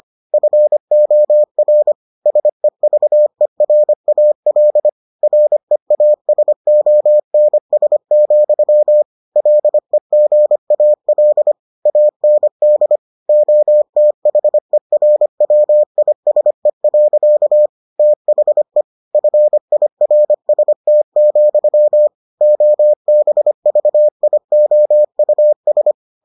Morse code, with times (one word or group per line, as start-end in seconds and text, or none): 0.3-1.9 FOR
2.3-4.9 SEVERAL
5.2-9.0 REASONS,
9.4-11.5 LEGAL
11.9-13.0 AND
13.3-17.7 OTHERWISE.
18.0-18.8 THE
19.1-22.1 FIRST,
22.4-25.9 OBVIOUS